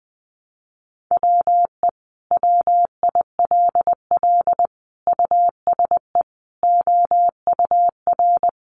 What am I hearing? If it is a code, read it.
Morse code, heard as a keyed tone, about 20 wpm, 700 Hz. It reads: WE WILL USE OUR
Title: Sentences from Top 1100 Words 20wpm